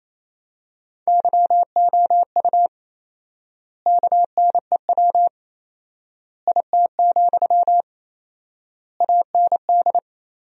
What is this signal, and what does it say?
Morse code, heard as a keyed tone, about 28 wpm, 700 Hz. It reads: YOU KNEW IT, AND
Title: 4x Word-Spacing - Sentences from Top 500 Words - 28wpm